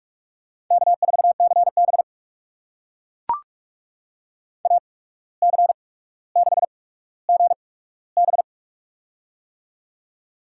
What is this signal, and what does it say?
Morse code, heard as a keyed tone, about 44 wpm, spaced as for 18 wpm, 700 Hz. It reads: K4CB AC6GB
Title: USA Call Signs - 8x Character Spacing - 45wpm